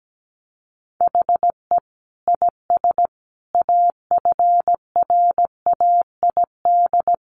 Morse code, heard as a keyed tone, about 17 wpm, 700 Hz. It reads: HE IS AFRAID
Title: Sentences from Top 1500 Words 17wpm